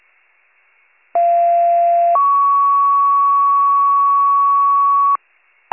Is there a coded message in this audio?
Yes: two-tone page from 1.1 to 5.2 s, heard as a one-second tone, then a three-second tone, about 690 then 1120 Hz, with noise.